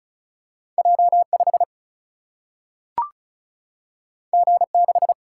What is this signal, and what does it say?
Morse code, heard as a keyed tone, about 36 wpm, 700 Hz. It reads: J 5 G 6
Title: Single-Letter Single-Number 35wpm